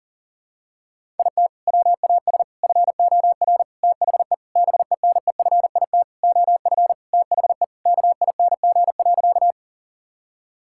Morse code, heard as a keyed tone, about 40 wpm, 700 Hz. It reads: IT WAS FOR THE BENEFIT OF THE KING.
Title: Sentences from Top 1000 Words - Review Part 6 - 40wpm